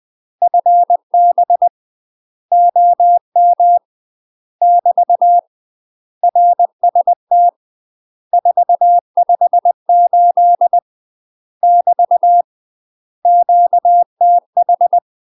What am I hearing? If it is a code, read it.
Morse code, heard as a keyed tone, about 20 wpm, 700 Hz. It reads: FB OM <BT> RST 458 <BT> QTH
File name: UK MASTER QSOs TRAINING / 20WPM / UK MASTER QSOs 0012